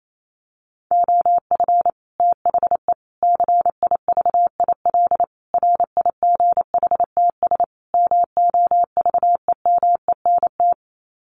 Morse code, heard as a keyed tone, about 28 wpm, 700 Hz. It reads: OF THE CIVIL RIGHTS MOVEMENT
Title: Sets of 5 Words Encore2 28wpm